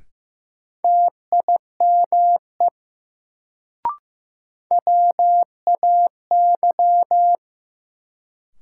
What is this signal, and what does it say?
Morse code, heard as a keyed tone, about 15 wpm, 700 Hz. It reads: TIME WAY
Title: Sentences from Top 100 Words - New Word Drill - 15wpm